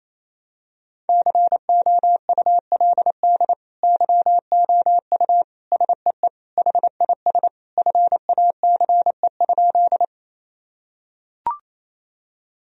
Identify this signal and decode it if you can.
Morse code, heard as a keyed tone, about 28 wpm, 700 Hz. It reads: COULD YOU SEE HIS FACE?
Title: Sentences from Top 1000 Words - Review Part 2 - 28wpm